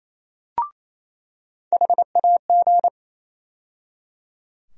Morse code, heard as a keyed tone, about 28 wpm, 700 Hz. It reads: HAG
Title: USA Call Sign Postfixes - 3 Letters 28wpm